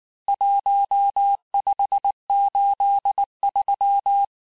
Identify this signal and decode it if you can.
Morse code, heard as a keyed tone, about 19 wpm, 800 Hz. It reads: 1583
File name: cours-du-lundi files / cours-1583--texte-chiffre-18-mpm